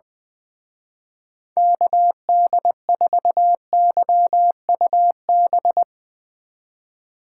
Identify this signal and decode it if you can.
Morse code, heard as a keyed tone, about 22 wpm, 700 Hz. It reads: KD4YUB
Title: Speed Racing - USA 2x3 Call Signs 20wpm